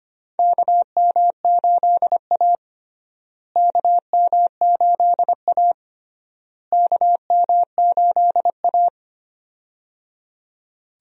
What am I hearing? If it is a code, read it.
Morse code, heard as a keyed tone, about 25 wpm, 700 Hz. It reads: KM8A KM8A KM8A
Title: USA Call Signs - 4 Characters - Repeated 3x - 25wpm